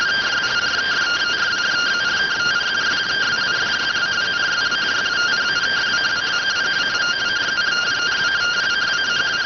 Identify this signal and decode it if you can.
Two-tone teleprinter signal, about 170 Hz shift.